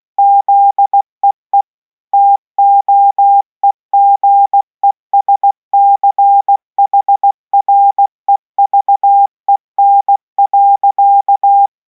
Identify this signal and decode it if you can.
Morse code, heard as a keyed tone, about 16 wpm, 800 Hz. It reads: ZEE TOEGESCHREVEN.